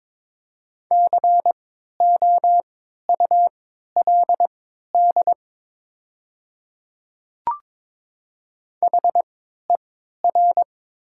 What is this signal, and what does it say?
Morse code, heard as a keyed tone, about 23 wpm, spaced as for 13 wpm, 700 Hz. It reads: COULD HER
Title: 3x Character Spacing - Top 100 Words - 22wpm